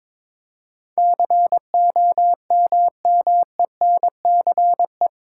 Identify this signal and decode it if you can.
Morse code, heard as a keyed tone, about 22 wpm, 700 Hz. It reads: COMMENCE
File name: practice / 22wpm / Eight Letter Words - 22wpm